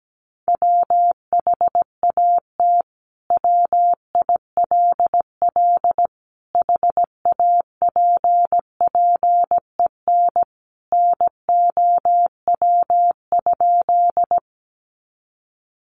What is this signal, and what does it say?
Morse code, heard as a keyed tone, about 17 wpm, 700 Hz. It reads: WHAT WILL HAPPEN NOW?
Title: Sentences from Top 400 Words 17wpm